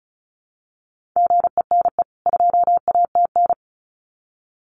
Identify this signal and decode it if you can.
Morse code, heard as a keyed tone, about 34 wpm, 700 Hz. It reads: GENE 2ATN
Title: CWT Exchange 35wpm